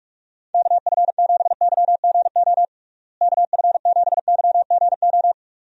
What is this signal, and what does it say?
Morse code, heard as a keyed tone, about 42 wpm, 700 Hz. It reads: KF7YGO KF7YGO
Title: USA Call Signs - 6 Characters - Repeated 2x - 45wpm